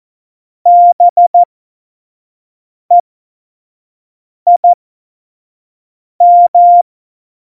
Morse code, heard as a keyed tone, about 14 wpm, spaced as for 5 wpm, 700 Hz. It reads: BEIM
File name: Morsekurs / Witze / Witz 09. Oktober 2018, 25 F70